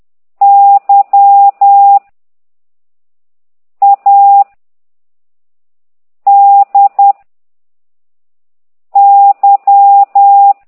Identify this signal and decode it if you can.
Morse code, heard as a keyed tone, about 10 wpm, 800 Hz. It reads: Y A D Y
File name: ct2idw / exerc9